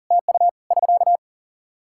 Morse code, heard as a keyed tone, about 40 wpm, 700 Hz. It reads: TU <SK>
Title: K1USN SST QSO Practice 40wpm